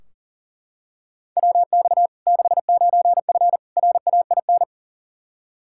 Morse code, heard as a keyed tone, about 40 wpm, 700 Hz. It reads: WX 69F RAIN
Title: Practice QSO - Exchange 2 - Info - 40wpm